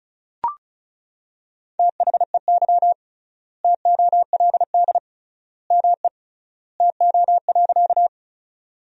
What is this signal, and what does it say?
Morse code, heard as a keyed tone, about 34 wpm, 700 Hz. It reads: THEY TOLD ME TO.
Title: 3x Word-Spacing - Sentences from Top 400 Words - 35wpm